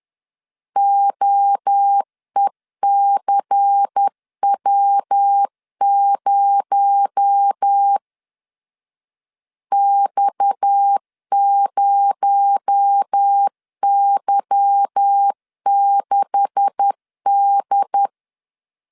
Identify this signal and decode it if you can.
Morse code, heard as a keyed tone, about 10 wpm, 800 Hz. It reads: OECW0 X0Y6D
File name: CW / DET / 1 9 1500C S B 10wpm 4439